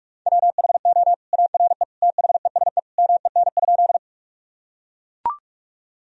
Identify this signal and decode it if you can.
Morse code, heard as a keyed tone, about 44 wpm, 700 Hz. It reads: WHO ARE THESE MEN?
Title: Sentences from Top 1000 Words - Review Part 1 - 45wpm